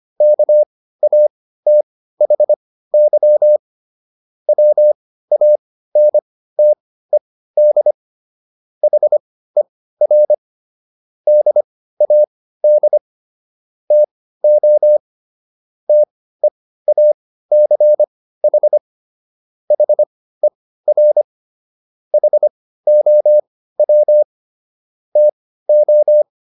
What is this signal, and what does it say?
Morse code, heard as a keyed tone, about 25 wpm, spaced as for 15 wpm, 600 Hz. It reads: KATHY WANTED HER DAD TO TEACH HER HOW TO